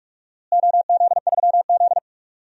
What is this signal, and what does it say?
Morse code, heard as a keyed tone, about 43 wpm, 700 Hz. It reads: OZ3Z
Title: DXCC-100 Call Signs - 4 Characters - Repeated 2x - 45wpm